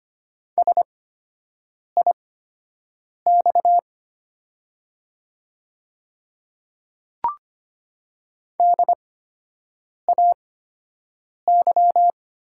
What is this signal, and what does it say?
Morse code, heard as a keyed tone, about 25 wpm, spaced as for 7 wpm, 700 Hz. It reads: SIX DAY
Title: 8x Character Spacing - Top 500 Words - 25wpm